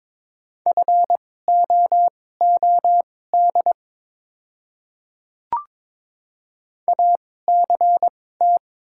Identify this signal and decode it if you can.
Morse code, heard as a keyed tone, about 22 wpm, spaced as for 16 wpm, 700 Hz. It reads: FOOD ACT
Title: 2x Character Spacing - Top 300 Words - 22wpm